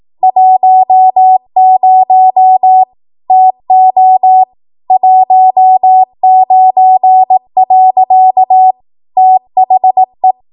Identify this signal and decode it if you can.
Morse code, heard as a keyed tone, about 18 wpm, 750 Hz. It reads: 10 TO 19. THE
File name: Morse / 18WPM